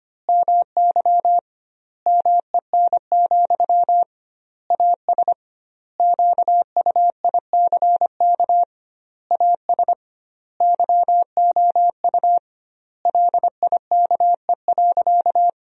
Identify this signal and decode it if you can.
Morse code, heard as a keyed tone, about 25 wpm, 700 Hz. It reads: MY MEN, AS QUICK AS YOU LIKE.
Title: 2x Word-Spacing - Sentences from Top 500 Words - 25wpm